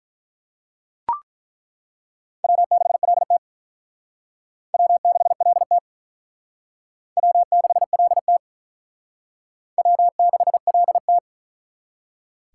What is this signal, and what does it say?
Morse code, heard as a keyed tone, about 43 wpm, 700 Hz. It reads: W6LT W6LT W6LT W6LT
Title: Speed Racing - USA 1x2 Call Signs 35wpm